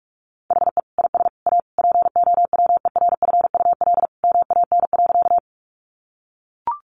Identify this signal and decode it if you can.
Morse code, heard as a keyed tone, about 44 wpm, 700 Hz. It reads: HE IS A POWERFUL MAN.
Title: Sentences from Top 1300 Words 45wpm